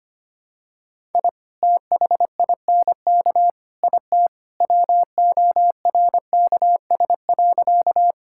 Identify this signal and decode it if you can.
Morse code, heard as a keyed tone, about 25 wpm, 700 Hz. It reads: I THINK IT WORKS.
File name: practice / 25wpm / Sentences with 2-Word Phrases - 25wpm